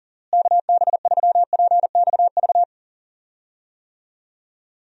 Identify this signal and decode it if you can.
Morse code, heard as a keyed tone, about 40 wpm, 700 Hz. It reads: KB3PXV
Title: USA Call Signs - 6 Characters - Repeated 3x - 40wpm